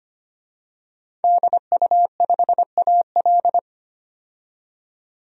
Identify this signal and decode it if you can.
Morse code, heard as a keyed tone, about 25 wpm, 700 Hz. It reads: DU5AL
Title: DXCC-100 Call Signs - 5 Characters - 25wpm